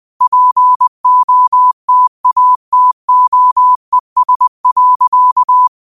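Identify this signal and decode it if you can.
Morse code, heard as a keyed tone, about 20 wpm, 1000 Hz. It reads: POTATOES.
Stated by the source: Morse quote practice at 20 wpm.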